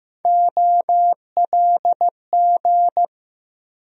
Morse code, heard as a keyed tone, about 15 wpm, 700 Hz. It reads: OLG
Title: USA Call Sign Postfixes - 3 Letters 15wpm